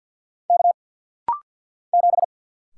Morse code, heard as a keyed tone, about 50 wpm, 690 Hz. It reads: K 7
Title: Single Letter-Number Alt - Rapid-Fire 50wpm